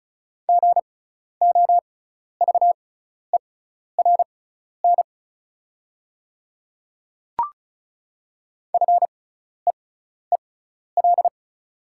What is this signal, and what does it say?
Morse code, heard as a keyed tone, about 36 wpm, spaced as for 12 wpm, 690 Hz. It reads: GOVERN FEEL
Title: 6x Character Spacing - Top 500 Words - 35wpm